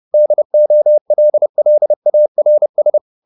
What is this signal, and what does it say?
Morse code, heard as a keyed tone, about 30 wpm, 600 Hz. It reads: DOLLARS
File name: the old man and the sea / the old man and the sea 1 30wpm